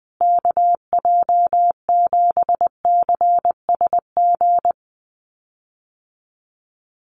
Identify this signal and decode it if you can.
Morse code, heard as a keyed tone, about 20 wpm, 700 Hz. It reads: KJ7CSG
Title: USA Call Signs Repeated 2x 20wpm